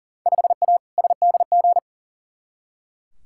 Morse code, heard as a keyed tone, about 37 wpm, 700 Hz. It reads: 5A SDG
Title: Field Day Exchanges Encore - 40wpm